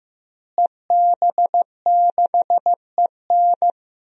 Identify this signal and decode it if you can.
Morse code, heard as a keyed tone, about 15 wpm, 700 Hz. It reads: EB6EN
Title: DXCC-100 Call Signs - 5 Characters - Repeated 2x - 15wpm